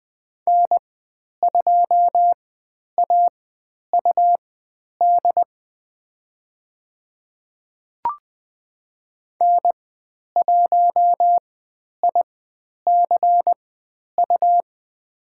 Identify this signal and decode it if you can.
Morse code, heard as a keyed tone, about 20 wpm, spaced as for 10 wpm, 700 Hz. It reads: N2AUD N1ICU
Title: 102 Lesson 17g - Letter u - Call Signs 20wpm